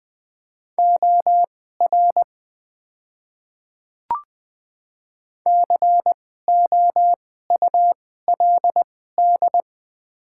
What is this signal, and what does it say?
Morse code, heard as a keyed tone, about 20 wpm, spaced as for 15 wpm, 700 Hz. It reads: OR COULD
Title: 2x Character Spacing - Top 100 Words - 20wpm